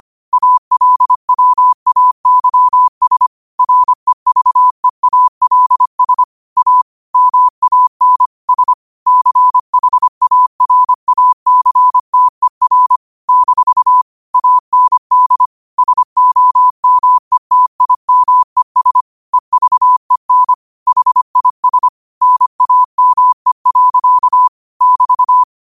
Morse code, heard as a keyed tone, about 25 wpm, 1000 Hz. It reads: ALWAYS REVEALS A MAN S CHARACTER - AND SOMETIMES EVEN HIS NAME. <BT>